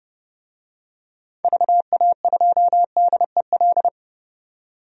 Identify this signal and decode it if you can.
Morse code, heard as a keyed tone, about 30 wpm, 700 Hz. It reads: VA2DEL
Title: DXCC-100 Call Signs - 6 Characters - 30wpm